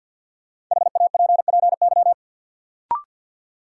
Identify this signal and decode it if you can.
Morse code, heard as a keyed tone, about 49 wpm, 700 Hz. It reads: HAPPY
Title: Sentences from Top 1000 Words - New Word Drill - 50wpm